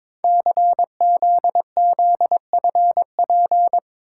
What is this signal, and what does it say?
Morse code, heard as a keyed tone, about 22 wpm, 700 Hz. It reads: CZZFP
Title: Code Groups - 5 Letter:Numbers - 22wpm